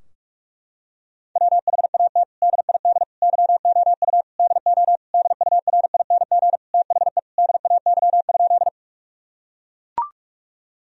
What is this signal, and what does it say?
Morse code, heard as a keyed tone, about 41 wpm, 700 Hz. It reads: WHAT DID YOU DO DURING THE DAY?
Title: Sentences from Top 1000 Words - Review Part 2 - 45wpm